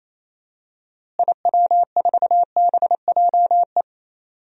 Morse code, heard as a keyed tone, about 28 wpm, 700 Hz. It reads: IW4BJE
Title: DXCC-100 Call Signs - 6 Characters - Repeated 2x - 28wpm